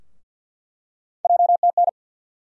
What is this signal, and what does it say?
Morse code, heard as a keyed tone, about 48 wpm, 700 Hz. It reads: JTN
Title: USA Call Sign Postfixes - 3 Letters 50wpm